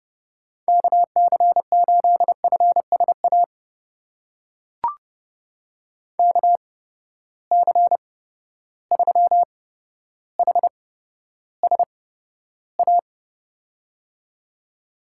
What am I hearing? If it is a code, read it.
Morse code, heard as a keyed tone, about 30 wpm, spaced as for 13 wpm, 700 Hz. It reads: KC8FSA KC3HSA
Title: USA Call Signs - 6 Characters - 8x Character Spacing - 30wpm